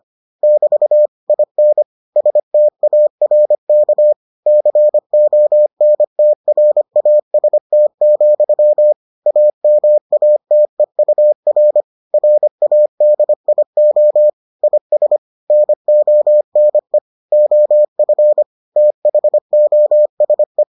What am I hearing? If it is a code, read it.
Morse code, heard as a keyed tone, about 25 wpm, 600 Hz. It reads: <BT> IN STARK CONTRAST, AMATEUR RADIO IS NONE OF THOSE